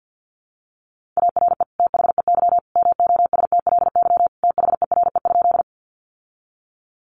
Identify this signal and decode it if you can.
Morse code, heard as a keyed tone, about 50 wpm, 700 Hz. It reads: ARE THEY MOSTLY THERE?
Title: Sentences from Top 1300 Words 50wpm